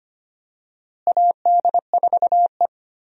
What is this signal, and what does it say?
Morse code, heard as a keyed tone, about 25 wpm, 700 Hz. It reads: AD4E
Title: USA Call Signs - 4 Characters - Repeated 2x - 25wpm